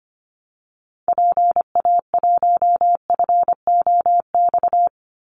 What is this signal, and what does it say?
Morse code, heard as a keyed tone, about 25 wpm, 700 Hz. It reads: PA1FOX
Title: Call Signs Repeated 2x - 25wpm